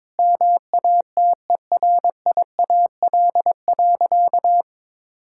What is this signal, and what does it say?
Morse code, heard as a keyed tone, about 22 wpm, 700 Hz. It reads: MATERIAL.
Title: Sentences from Top 1000 Words - Review Part 5 - 22wpm